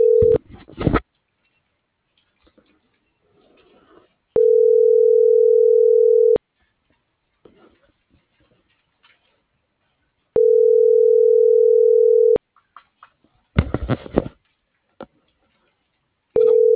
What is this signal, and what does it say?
Ringback tone (North America), 440 + 480 Hz.